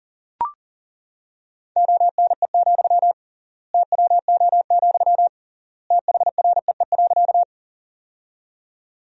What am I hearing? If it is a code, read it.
Morse code, heard as a keyed tone, about 40 wpm, 700 Hz. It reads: ONE, TWO, THREE.
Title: 3x Word-Spacing - Sentences from Top 200 Words - 40wpm